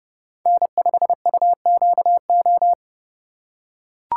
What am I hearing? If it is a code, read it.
Morse code, heard as a keyed tone, about 30 wpm, 700 Hz. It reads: N5UQO
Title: USA Call Signs - 5 Characters - 4x Character Spacing - 30wpm